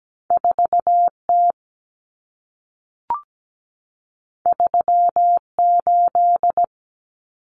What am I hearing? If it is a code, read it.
Morse code, heard as a keyed tone, about 17 wpm, 700 Hz. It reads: 4T 38
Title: Common Cut Numbers - 2-Digit 17wpm